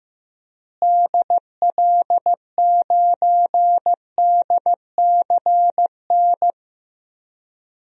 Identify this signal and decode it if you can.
Morse code, heard as a keyed tone, about 15 wpm, 700 Hz. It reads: DL9DCN
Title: DXCC-100 Call Signs - 6 Characters - Repeated 3x - 15wpm